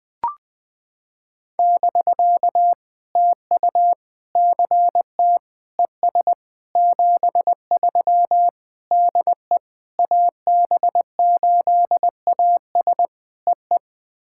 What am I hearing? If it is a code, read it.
Morse code, heard as a keyed tone, about 20 wpm, 700 Hz. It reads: <BK> TU CT ES 73 DE AB8AS EE
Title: POTA QSO Practice - 20wpm